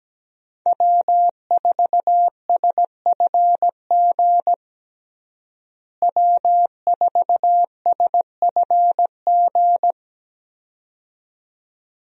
Morse code, heard as a keyed tone, about 17 wpm, 700 Hz. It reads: W4SFG W4SFG
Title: USA Call Signs - 5 Characters - Repeated 2x - 17wpm